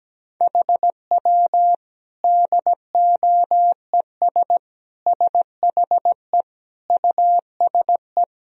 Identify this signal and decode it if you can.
Morse code, heard as a keyed tone, about 17 wpm, 700 Hz. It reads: HW DOES SHE USE
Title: Sentences from Top 200 Words and Abbreviations 17wpm